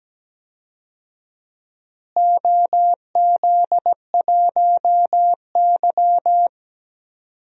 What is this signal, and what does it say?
Morse code, heard as a keyed tone, about 17 wpm, 700 Hz. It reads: OZ1Y
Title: DXCC-100 Call Signs - 4 Characters - Repeated 3x - 17wpm